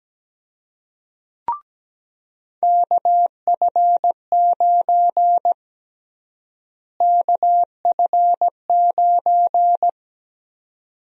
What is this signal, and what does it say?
Morse code, heard as a keyed tone, about 17 wpm, 700 Hz. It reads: KF9 KF9
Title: USA Call Sign Prefixes - Repeated 2x - 17wpm